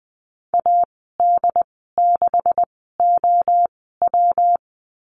Morse code, heard as a keyed tone, about 20 wpm, spaced as for 15 wpm, 700 Hz. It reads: AD6OW
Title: 128 Lesson 21g - Number 6 - Call Signs 20wpm